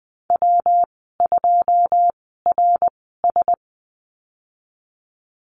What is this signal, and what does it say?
Morse code, heard as a keyed tone, about 19 wpm, spaced as for 14 wpm, 700 Hz. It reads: W2RS
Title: 128 Lesson 21g - Number 6 - Call Signs 20wpm